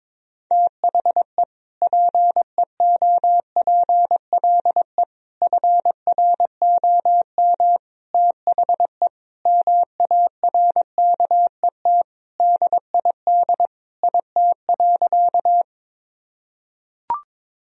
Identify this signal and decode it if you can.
Morse code, heard as a keyed tone, about 22 wpm, 700 Hz. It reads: THE PEOPLE FROM THE MARKET DID IT.
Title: Sentences from Top 1000 Words - Review Part 3 - 22wpm